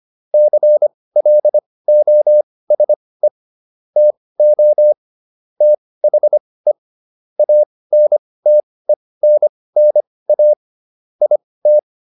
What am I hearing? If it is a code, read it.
Morse code, heard as a keyed tone, about 25 wpm, spaced as for 18 wpm, 600 Hz. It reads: CLOSE TO THE ANTENNA IT